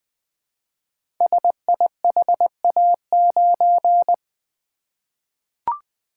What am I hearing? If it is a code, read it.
Morse code, heard as a keyed tone, about 20 wpm, 700 Hz. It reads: SIHA9
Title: Code Groups - 5 Letter:Numbers - 20wpm